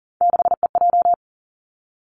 Morse code, heard as a keyed tone, about 38 wpm, 690 Hz. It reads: 6EJ
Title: Three Challenging Letter-Number Combinations 40wpm